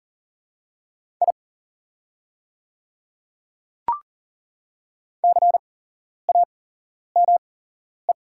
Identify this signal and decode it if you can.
Morse code, heard as a keyed tone, about 40 wpm, spaced as for 11 wpm, 700 Hz. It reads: I CAME
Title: 8x Character Spacing - Top 200 Words - 40wpm